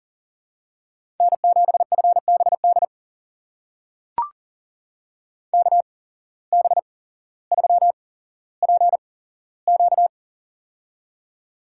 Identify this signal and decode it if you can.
Morse code, heard as a keyed tone, about 39 wpm, spaced as for 17 wpm, 700 Hz. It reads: N7FBD KB3PQ
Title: USA Call Signs - 5 Characters - 8x Character Spacing - 40wpm